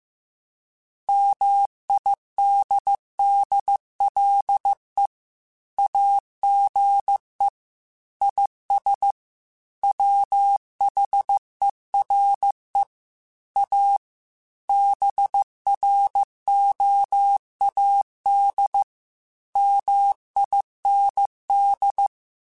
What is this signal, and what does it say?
Morse code, heard as a keyed tone, about 15 wpm, 800 Hz. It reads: MIDDLE AGE IS WHERE A BROAD MIND
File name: Chuckles / Chuckles-mp3s / MiddleAge @13